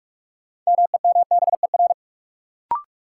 Morse code, heard as a keyed tone, about 46 wpm, 700 Hz. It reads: MEMBER
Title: Sets of 3 Words - New Word Drill - 45wpm